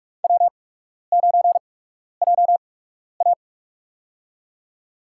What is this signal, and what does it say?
Morse code, heard as a keyed tone, about 44 wpm, spaced as for 12 wpm, 700 Hz. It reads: W9JA
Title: USA Call Signs - 4 Characters - 8x Character Spacing - 45wpm